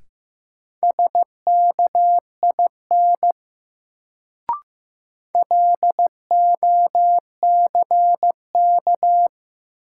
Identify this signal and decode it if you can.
Morse code, heard as a keyed tone, about 15 wpm, 700 Hz. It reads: SKIN LOCK
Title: Sentences from Top 1300 Words - New Word Drill Alt - 15wpm